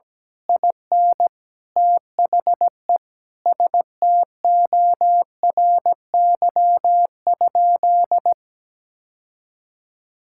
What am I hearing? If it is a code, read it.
Morse code, heard as a keyed tone, about 17 wpm, 700 Hz. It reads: IN THE STORY?
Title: Sentences from Top 1000 Words - Review Part 5 - 17wpm